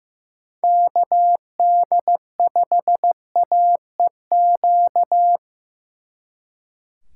Morse code, heard as a keyed tone, about 18 wpm, 700 Hz. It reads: KD5AEQ
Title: Speed Racing - USA 2x3 Call Signs 15wpm